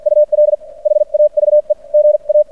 Morse code, heard as a keyed tone, about 56 wpm, spaced as for 44 wpm, 600 Hz. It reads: UL HAVE GA
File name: archived / CRSnet